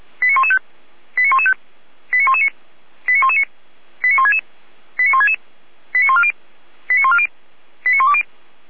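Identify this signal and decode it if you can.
Five-tone selective call (ZVEI), heard as quick runs of five tones, tones of about 70 ms. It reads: 89115 89115 89119 89119 89166 89166 89140 89140 89120
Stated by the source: ZVEI five-tone selective call 89078 received on SDR.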